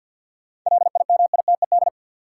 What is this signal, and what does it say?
Morse code, heard as a keyed tone, about 53 wpm, 700 Hz. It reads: LIMITED
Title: Sentences from Top 900 Words - New Word Drill - 50wpm